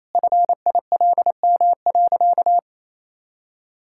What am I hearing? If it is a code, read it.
Morse code, heard as a keyed tone, about 28 wpm, 700 Hz. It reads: FILM.
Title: Sentences from Top 1000 Words - Review Part 5 - 28wpm